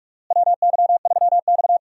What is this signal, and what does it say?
Morse code, heard as a keyed tone, about 46 wpm, 700 Hz. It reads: WY3X
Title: USA Call Signs - 4 Characters - Repeated 2x - 45wpm